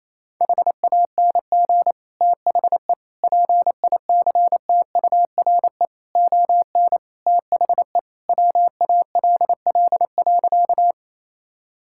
Morse code, heard as a keyed tone, about 28 wpm, 700 Hz. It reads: HANG THE PICTURE ON THE WALL.
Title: Sentences from Top 1600 Words 28wpm